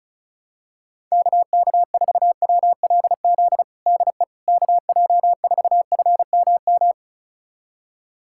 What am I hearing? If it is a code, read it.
Morse code, heard as a keyed tone, about 36 wpm, 700 Hz. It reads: KK4WLZ DE KJ4FMM
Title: USA Callsign DE Callsign - 6 Characters - 35wpm